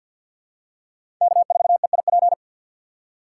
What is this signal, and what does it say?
Morse code, heard as a keyed tone, about 51 wpm, 700 Hz. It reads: K4EIP
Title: USA Call Signs - 2x Character Spacing - 50wpm